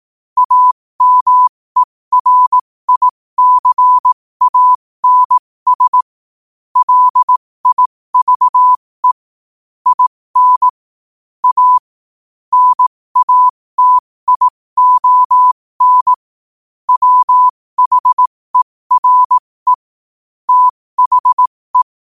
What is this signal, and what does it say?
Morse code, heard as a keyed tone, about 18 wpm, spaced as for 15 wpm, 1000 Hz. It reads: AMERICANS LIVE IN A NATION WHERE THE
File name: qotdfiles / 15wpm